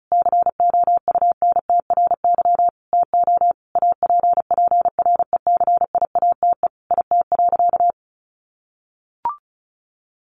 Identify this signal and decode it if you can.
Morse code, heard as a keyed tone, about 34 wpm, 700 Hz. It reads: COUNTRY TO APPRECIATE IT.
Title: Sentences from Top 1300 Words 35wpm